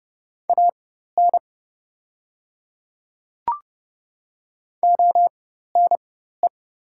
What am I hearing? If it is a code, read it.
Morse code, heard as a keyed tone, about 30 wpm, spaced as for 14 wpm, 700 Hz. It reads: AN ONE
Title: 4x Character Spacing - Top 100 Words - 30wpm